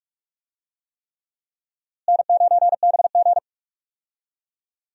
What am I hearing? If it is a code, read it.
Morse code, heard as a keyed tone, about 41 wpm, 700 Hz. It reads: N9BG
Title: USA Call Signs - 4 Characters - Repeated 3x - 45wpm